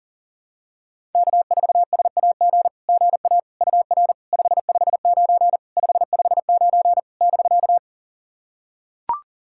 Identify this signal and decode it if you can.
Morse code, heard as a keyed tone, about 40 wpm, 700 Hz. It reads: K4SAG GA UR 559 559 <BK>
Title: SOTA QSO Practice 40wpm